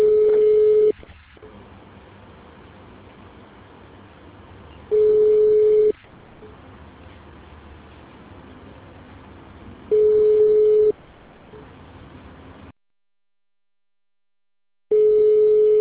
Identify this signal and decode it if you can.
Ringback tone (Europe, 425 Hz).